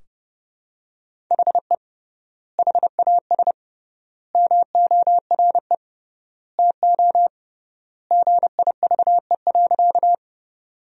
Morse code, heard as a keyed tone, about 32 wpm, 700 Hz. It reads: HE HAS MORE TO GIVE.
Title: 3x Word-Spacing - Sentences from Top 200 Words - 30wpm